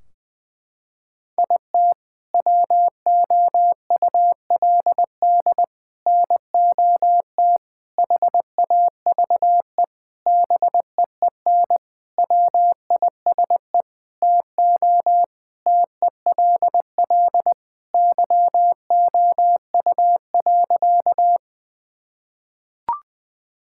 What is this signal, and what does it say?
Morse code, heard as a keyed tone, about 20 wpm, 700 Hz. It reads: IT WOULD NOT HAVE BEEN WISE TO TELL YOU.
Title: Sentences from Top 1600 Words 20wpm